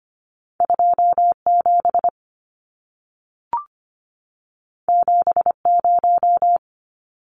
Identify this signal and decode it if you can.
Morse code, heard as a keyed tone, about 25 wpm, 700 Hz. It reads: 27 70
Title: Numbers - 2-Digit 25wpm